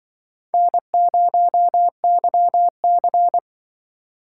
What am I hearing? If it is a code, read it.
Morse code, heard as a keyed tone, about 25 wpm, 700 Hz. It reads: N0YC
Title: Speed Racing - USA Call Signs 20wpm